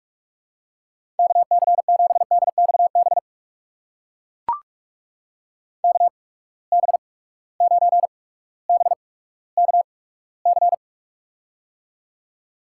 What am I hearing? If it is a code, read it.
Morse code, heard as a keyed tone, about 44 wpm, spaced as for 19 wpm, 700 Hz. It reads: KC7DXB KB9BKC
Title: USA Call Signs - 6 Characters - 8x Character Spacing - 45wpm